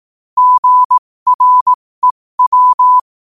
Morse code, heard as a keyed tone, about 18 wpm, spaced as for 15 wpm, 1000 Hz. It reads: GREW
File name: qotdfiles / 15wpm